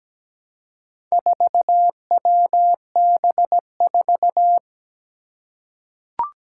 Morse code, heard as a keyed tone, about 17 wpm, 700 Hz. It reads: 4WB4
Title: Four Challenging Letter-Number Combinations 17wpm